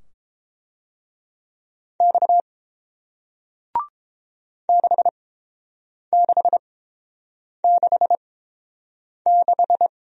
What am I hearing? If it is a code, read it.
Morse code, heard as a keyed tone, about 30 wpm, spaced as for 7 wpm, 700 Hz. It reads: X 6666